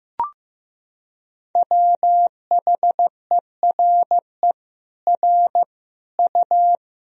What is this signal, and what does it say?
Morse code, heard as a keyed tone, about 16 wpm, 700 Hz. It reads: WHERE R U
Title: Sentences from Top 200 Words and Abbreviations 15wpm